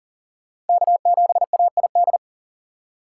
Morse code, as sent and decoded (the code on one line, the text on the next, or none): -.- --... .- .. -..
K7AID